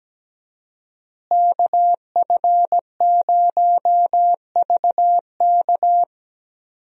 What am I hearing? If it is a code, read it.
Morse code, heard as a keyed tone, about 17 wpm, 700 Hz. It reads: KF0VK
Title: USA Call Signs - 5 Characters - Repeated 3x - 17wpm